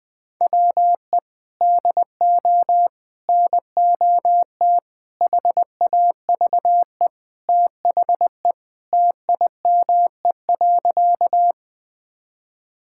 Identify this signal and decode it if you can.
Morse code, heard as a keyed tone, about 19 wpm, 700 Hz. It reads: WE DO NOT HAVE THE TIME.
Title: Sentences with 5-Word Phrases - 20wpm